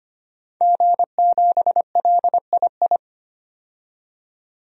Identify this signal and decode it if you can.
Morse code, heard as a keyed tone, about 25 wpm, 700 Hz. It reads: G7LII
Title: DXCC-100 Call Signs - 5 Characters - 25wpm